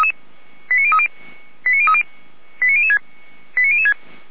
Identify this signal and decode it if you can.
Five-tone selective call (ZVEI), heard as quick runs of five tones, tones of about 70 ms. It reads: 89030 89030 89006 89006